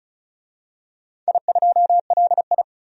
Morse code, heard as a keyed tone, about 34 wpm, 700 Hz. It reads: I2LI
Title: DXCC-100 Call Signs - 4 Characters - 35wpm